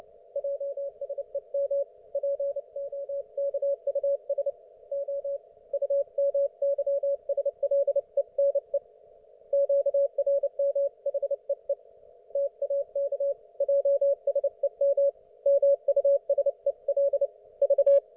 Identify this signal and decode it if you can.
Morse code, heard as a keyed tone, about 29 wpm, 560 Hz. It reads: JSEM POKUS O UMYSLENE QRM HEE TAK JSEM MUSEL V